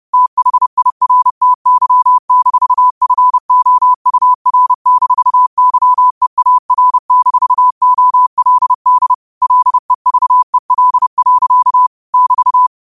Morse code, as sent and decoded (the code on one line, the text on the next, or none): - .... .. .-. - -.-- -....- ..-. --- ..- .-. -....- -.-- . .- .-. -....- --- .-.. -.. / .-.. ...- . .-.. .-.-.- / -...-
THIRTY-FOUR-YEAR-OLD LEVEL. <BT>